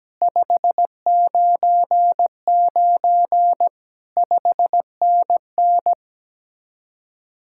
Morse code, heard as a keyed tone, about 17 wpm, 700 Hz. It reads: 599 5NN
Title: Speed-Racing Common QSO Phrases - 17wpm